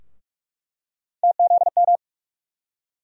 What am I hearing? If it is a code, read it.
Morse code, heard as a keyed tone, about 47 wpm, 700 Hz. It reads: TZM
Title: USA Call Sign Postfixes - 3 Letters 45wpm